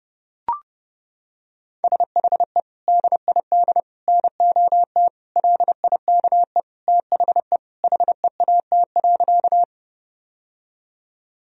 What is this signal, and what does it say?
Morse code, heard as a keyed tone, about 30 wpm, 700 Hz. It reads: SHE DID NOT LIKE THE HEAT.
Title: Sentences from Top 1000 Words - Review Part 3 - 30wpm